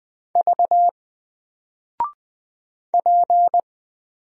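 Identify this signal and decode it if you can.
Morse code, heard as a keyed tone, about 20 wpm, 700 Hz. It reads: V P
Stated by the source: Single letter and number Morse drill at 20 wpm, from N4LCD.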